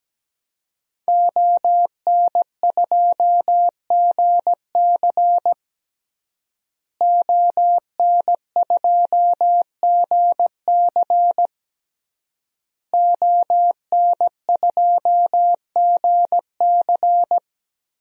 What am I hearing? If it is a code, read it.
Morse code, heard as a keyed tone, about 17 wpm, 700 Hz. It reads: ON2GC ON2GC ON2GC
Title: DXCC-100 Call Signs - 5 Characters - Repeated 3x - 17wpm